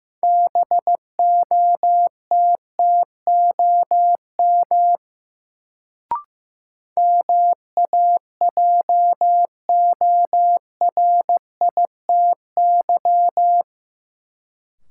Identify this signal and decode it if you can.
Morse code, heard as a keyed tone, about 15 wpm, 700 Hz. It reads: BOTTOM MAJORITY
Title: Sets of 3 Words - New Word Drill - 15wpm